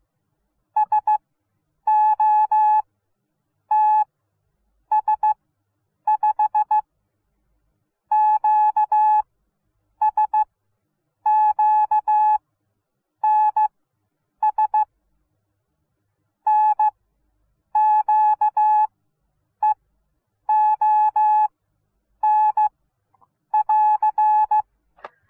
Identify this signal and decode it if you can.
Morse code, heard as a keyed tone, about 16 wpm, spaced as for 8 wpm, 860 Hz. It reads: SOTS5 QSQNS NQEON <AR>